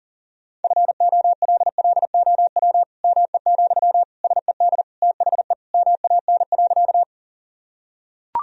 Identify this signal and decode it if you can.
Morse code, heard as a keyed tone, about 40 wpm, 700 Hz. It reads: FOLLOW ME, SED THE MAN.
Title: Sentences from Top 200 Words and Abbreviations 40wpm